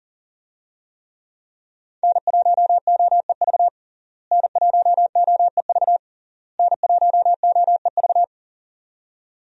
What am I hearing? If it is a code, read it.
Morse code, heard as a keyed tone, about 40 wpm, 700 Hz. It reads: N1OEV N1OEV N1OEV
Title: USA Call Signs - 5 Characters - Repeated 3x - 40wpm